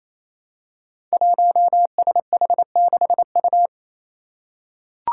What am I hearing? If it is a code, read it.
Morse code, heard as a keyed tone, about 27 wpm, 700 Hz. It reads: 1SH6U
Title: Five Challenging Letter-Number Combinations 28wpm